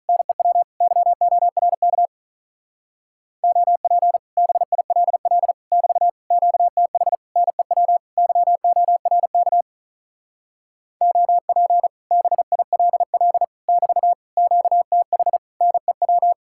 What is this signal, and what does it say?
Morse code, heard as a keyed tone, about 46 wpm, 700 Hz. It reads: NEW YORK OP BILL <BT> QTH NEW YORK OP BILL <BT> QTH NEW